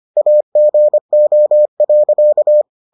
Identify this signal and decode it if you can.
Morse code, heard as a keyed tone, about 25 wpm, 600 Hz. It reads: AGO.